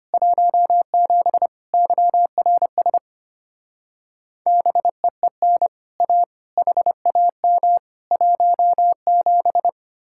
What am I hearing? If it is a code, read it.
Morse code, heard as a keyed tone, about 31 wpm, 700 Hz. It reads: 17 YRS BEEN A HAM 17
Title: Speed-Racing Common QSO Phrases - 25wpm